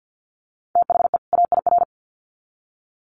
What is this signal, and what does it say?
Morse code, heard as a keyed tone, about 50 wpm, 700 Hz. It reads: THE AIR
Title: Sets of 2 Words 50wpm